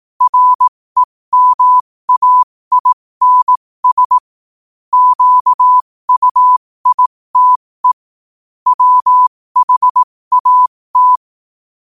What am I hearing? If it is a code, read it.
Morse code, heard as a keyed tone, about 18 wpm, spaced as for 15 wpm, 1000 Hz. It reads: REMAINS QUITE WHAT
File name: qotdfiles / 15wpm